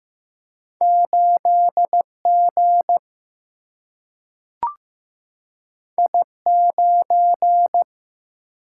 Morse code, heard as a keyed tone, about 15 wpm, 700 Hz. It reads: 8G I9
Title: Single-Letter Single-Number 15wpm